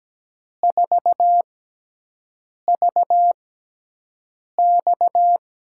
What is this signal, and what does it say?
Morse code, heard as a keyed tone, about 17 wpm, spaced as for 6 wpm, 700 Hz. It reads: 4VX
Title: USA Call Signs - 4 Characters - 6x Character Spacing - 17wpm